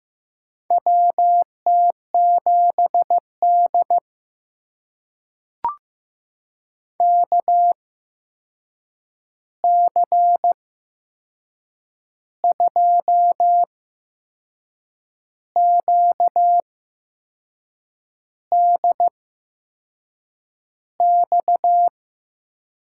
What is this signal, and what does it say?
Morse code, heard as a keyed tone, about 15 wpm, spaced as for 6 wpm, 700 Hz. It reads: WT7D KC2QDX
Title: USA Call Signs - 8x Character Spacing - 15wpm